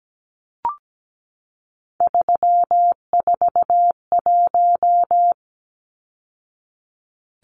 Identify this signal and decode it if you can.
Morse code, heard as a keyed tone, about 17 wpm, 700 Hz. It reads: 341
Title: Numbers - 3-Digit 17wpm